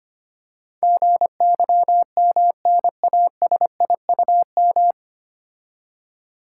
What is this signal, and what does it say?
Morse code, heard as a keyed tone, about 25 wpm, 690 Hz. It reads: GYMNASIUM
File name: practice / 25wpm / Difficult Words - Speed-Racing with Hints - 25wpm